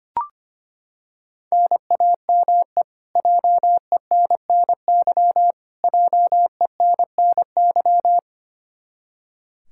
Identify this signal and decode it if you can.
Morse code, heard as a keyed tone, about 25 wpm, 700 Hz. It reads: NAME JENNY JENNY
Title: Practice QSO - Exchange 1 - Beginning - 25wpm